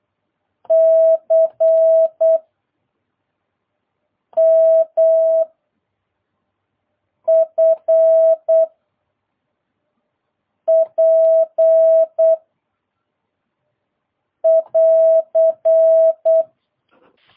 Morse code, heard as a keyed tone, about 8 wpm, 650 Hz. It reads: C M F P <AR>